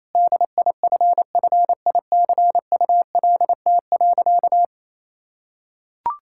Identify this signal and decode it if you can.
Morse code, heard as a keyed tone, about 28 wpm, 700 Hz. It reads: DIFFICULT.
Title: Sentences from Top 1600 Words 28wpm